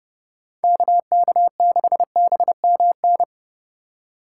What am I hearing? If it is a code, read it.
Morse code, heard as a keyed tone, about 34 wpm, 700 Hz. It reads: KK6BMN